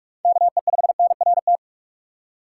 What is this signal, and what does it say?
Morse code, heard as a keyed tone, about 43 wpm, 700 Hz. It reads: KE5NRT
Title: USA Call Signs - 6 Characters - 6x Character Spacing - 45wpm